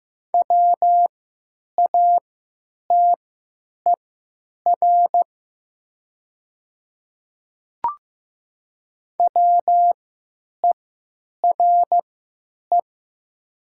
Morse code, heard as a keyed tone, about 15 wpm, spaced as for 8 wpm, 700 Hz. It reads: WATER WERE
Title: 3x Character Spacing - Top 200 Words - 15wpm